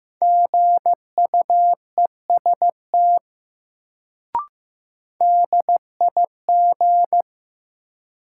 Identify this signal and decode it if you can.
Morse code, heard as a keyed tone, about 15 wpm, 700 Hz. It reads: GUEST DIG
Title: Sentences from Top 1600 Words - New Word Drill Alt - 15wpm